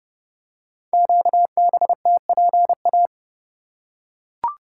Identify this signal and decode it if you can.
Morse code, heard as a keyed tone, about 30 wpm, 700 Hz. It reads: QBTPA
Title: Code Groups - 5 Letters - 30wpm